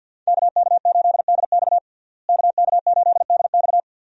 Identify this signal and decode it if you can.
Morse code, heard as a keyed tone, about 47 wpm, 700 Hz. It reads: KK8DX KK8DX
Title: USA Call Signs - 5 Characters - Repeated 2x - 50wpm